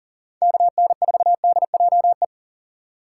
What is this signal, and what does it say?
Morse code, heard as a keyed tone, about 40 wpm, 700 Hz. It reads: KN4DJE
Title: USA Call Signs - 6 Characters - 4x Character Spacing - 40wpm